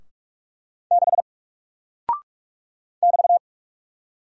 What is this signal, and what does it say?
Morse code, heard as a keyed tone, about 46 wpm, 700 Hz. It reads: 6 <BT>
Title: Single Character (Letter:Number:Punctuation:Prosign) Alt 45wpm